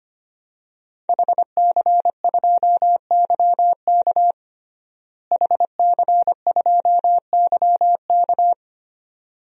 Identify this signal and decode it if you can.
Morse code, heard as a keyed tone, about 25 wpm, 700 Hz. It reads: HC2YK HC2YK
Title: DXCC-100 Call Signs - 5 Characters - Repeated 2x - 25wpm